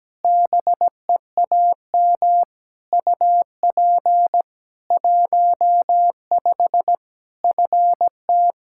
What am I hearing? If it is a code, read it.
Morse code, heard as a keyed tone, about 17 wpm, 700 Hz. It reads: BEAM UP 15 FT
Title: K1USN SST QSO Practice 17wpm